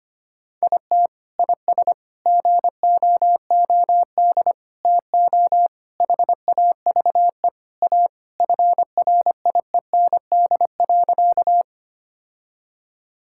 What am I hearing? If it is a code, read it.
Morse code, heard as a keyed tone, about 25 wpm, 700 Hz. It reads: IT IS GOOD TO HAVE A FRIEND.
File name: practice / 25wpm / Sentences with 3-Word Phrases - 25wpm